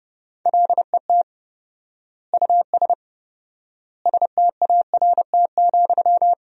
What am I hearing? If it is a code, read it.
Morse code, heard as a keyed tone, about 30 wpm, 700 Hz. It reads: LET US START,
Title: 4x Word-Spacing - Sentences from Top 300 Words - 30wpm